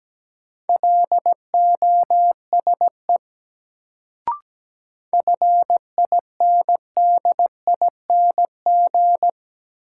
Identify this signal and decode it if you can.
Morse code, heard as a keyed tone, about 17 wpm, 700 Hz. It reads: LOSE FINDING
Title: Sentences from Top 1100 Words - New Word Drill - 17wpm